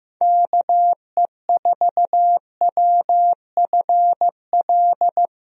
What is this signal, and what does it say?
Morse code, heard as a keyed tone, about 15 wpm, 700 Hz. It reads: KE4WFL
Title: USA Call Signs - 6 Characters - Repeated 3x - 15wpm